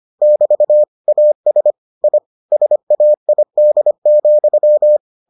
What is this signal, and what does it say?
Morse code, heard as a keyed tone, about 25 wpm, 600 Hz. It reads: <BT> AS I SAID,